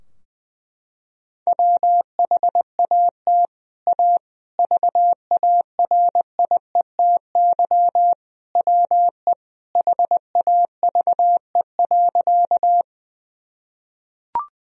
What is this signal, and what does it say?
Morse code, heard as a keyed tone, about 20 wpm, 700 Hz. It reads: WHAT A VARIETY WE HAVE.
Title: Sentences from Top 1000 Words - Review Part 4 - 20wpm